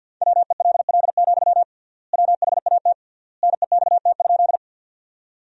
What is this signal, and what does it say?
Morse code, heard as a keyed tone, about 50 wpm, 700 Hz. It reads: WELL, WHAT NEXT?
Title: 3x Word-Spacing - Sentences from Top 300 Words - 50wpm